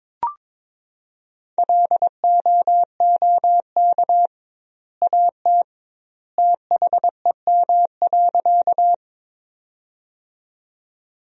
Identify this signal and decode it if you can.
Morse code, heard as a keyed tone, about 22 wpm, 700 Hz. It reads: LOOK AT THEM.
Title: 2x Word-Spacing - Sentences from Top 100 Words - 22wpm